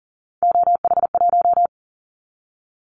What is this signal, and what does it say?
Morse code, heard as a keyed tone, about 40 wpm, 700 Hz. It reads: OH1
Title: Callsign Prefixes for Top 25 DXCC - 40wpm